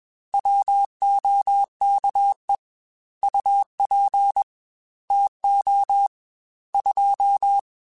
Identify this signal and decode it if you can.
Morse code, heard as a keyed tone, about 21 wpm, 800 Hz. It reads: WOKE UP TO 2
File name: Weather Sets / Weather Sets-mp3s / Weather 3 17 @20